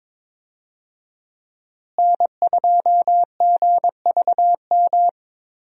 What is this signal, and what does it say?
Morse code, heard as a keyed tone, about 22 wpm, 700 Hz. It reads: N2GVM
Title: USA Call Signs - 5 Characters - Repeated 2x - 22wpm